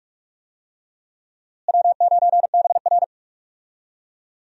Morse code, heard as a keyed tone, about 41 wpm, 700 Hz. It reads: W9BR
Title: USA Call Signs - 4 Characters - Repeated 3x - 45wpm